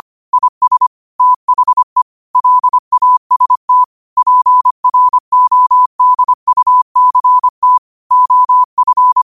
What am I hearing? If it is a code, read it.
Morse code, heard as a keyed tone, about 25 wpm, 1000 Hz. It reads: IS THE LAST PRODUCT OF
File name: qotdfiles / 25wpm